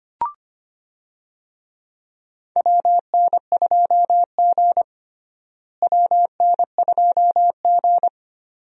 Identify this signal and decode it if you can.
Morse code, heard as a keyed tone, about 25 wpm, 700 Hz. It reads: WN2G WN2G
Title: USA Call Signs - 4 Characters - Repeated 2x - 25wpm